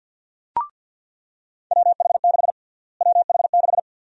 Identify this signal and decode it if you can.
Morse code, heard as a keyed tone, about 49 wpm, 700 Hz. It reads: WH6 WH6
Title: USA Call Sign Prefixes - Repeated 2x - 50wpm